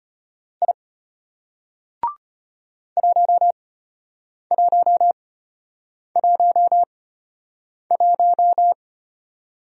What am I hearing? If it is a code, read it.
Morse code, heard as a keyed tone, about 30 wpm, spaced as for 8 wpm, 700 Hz. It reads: I 1111
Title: Speed-Racing Single Letter-Number 25wpm